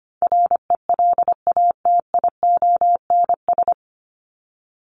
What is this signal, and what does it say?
Morse code, heard as a keyed tone, about 24 wpm, 700 Hz. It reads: RELATIONS
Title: ES:S Suffix - Top 100 Words with Base Words 25wpm